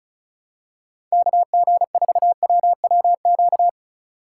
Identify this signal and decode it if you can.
Morse code, heard as a keyed tone, about 36 wpm, 700 Hz. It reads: KG4WWQ
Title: USA Call Signs - 6 Characters - 2x Character Spacing - 35wpm